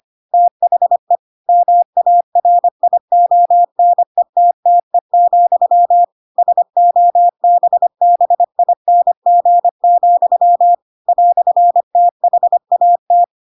Morse code, heard as a keyed tone, about 25 wpm, 700 Hz. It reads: THE MARIONETTE, SOBBING, "THAT